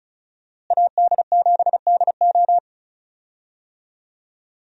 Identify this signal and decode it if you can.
Morse code, heard as a keyed tone, about 36 wpm, 700 Hz. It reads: AD7DO
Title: USA Call Signs - 5 Characters - Repeated 3x - 35wpm